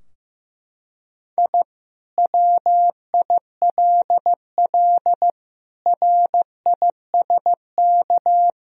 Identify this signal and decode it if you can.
Morse code, heard as a keyed tone, about 15 wpm, 700 Hz. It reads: I WILL RISK
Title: Sentences from Top 1000 Words - Review Part 4 - 15wpm